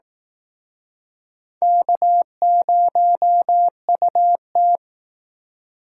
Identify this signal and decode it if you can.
Morse code, heard as a keyed tone, about 18 wpm, 700 Hz. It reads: K0UT
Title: Speed Racing - USA 1x2 Call Signs 15wpm